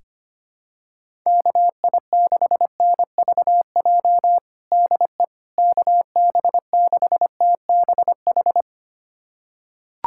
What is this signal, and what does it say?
Morse code, heard as a keyed tone, about 25 wpm, 700 Hz. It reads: KI6NVJ DE KB6TBH
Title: USA Callsign DE Callsign - 6 Characters - 25wpm